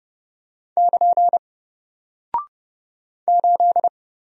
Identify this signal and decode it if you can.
Morse code, heard as a keyed tone, about 29 wpm, 700 Hz. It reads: <KN> 8
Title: Single Character (Letter:Number:Punctuation:Prosign) 30wpm